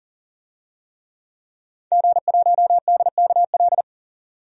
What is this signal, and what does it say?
Morse code, heard as a keyed tone, about 40 wpm, 700 Hz. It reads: G1DKL
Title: DXCC-100 Call Signs - 5 Characters - 40wpm